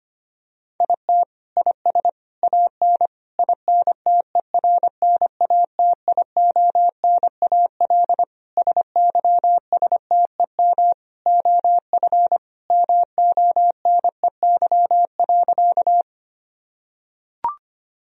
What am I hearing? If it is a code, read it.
Morse code, heard as a keyed tone, about 25 wpm, 700 Hz. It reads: IT IS AN INTERNATIONAL SYSTEM OF MONEY.
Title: Sentences from Top 1000 Words - Review Part 4 - 25wpm